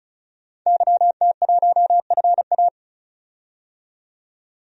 Morse code, heard as a keyed tone, about 35 wpm, 700 Hz. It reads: YT1FA
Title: DXCC-100 Call Signs - 5 Characters - 35wpm